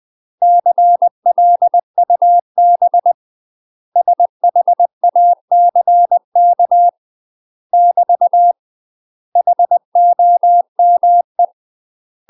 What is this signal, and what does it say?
Morse code, heard as a keyed tone, about 20 wpm, 700 Hz. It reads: CLUB SHACK <BT> HOME
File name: UK MASTER QSOs TRAINING / 20WPM / UK MASTER QSOs 0040